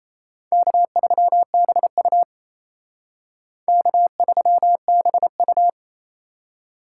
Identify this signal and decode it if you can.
Morse code, heard as a keyed tone, about 36 wpm, 700 Hz. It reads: K3BU K3BU